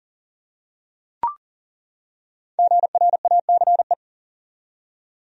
Morse code, heard as a keyed tone, about 40 wpm, 700 Hz. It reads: GRACE
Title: USA Common Names - Max 5 Letters - 40wpm